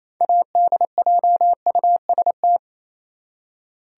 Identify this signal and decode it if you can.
Morse code, heard as a keyed tone, about 28 wpm, 700 Hz. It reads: ADJUST
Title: Six Letter Words - 28wpm